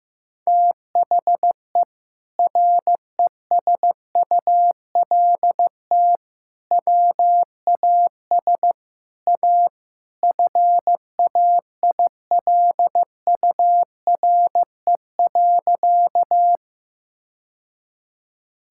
Morse code, heard as a keyed tone, about 15 wpm, 700 Hz. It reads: THE RESULT WAS A FAILURE.
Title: Sentences from Top 1000 Words - Review Part 6 - 15wpm